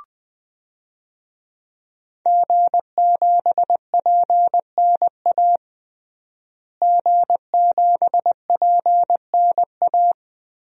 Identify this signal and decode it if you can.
Morse code, heard as a keyed tone, about 21 wpm, 700 Hz. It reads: G7PNA G7PNA
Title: DXCC-100 Call Signs - 5 Characters - Repeated 3x - 20wpm